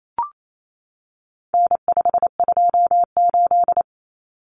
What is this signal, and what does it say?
Morse code, heard as a keyed tone, about 28 wpm, 700 Hz. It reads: N528
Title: Common Cut Numbers - 4-Digit 28wpm